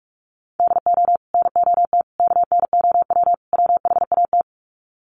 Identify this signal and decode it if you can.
Morse code, heard as a keyed tone, about 44 wpm, 700 Hz. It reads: DO NOT KNOW WHAT
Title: Sets of 4 Words 45wpm